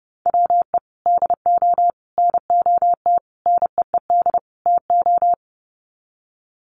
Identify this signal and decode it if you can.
Morse code, heard as a keyed tone, about 30 wpm, 700 Hz. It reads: WE DO NOT NEED TO
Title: Sets of 5 Words Encore 30wpm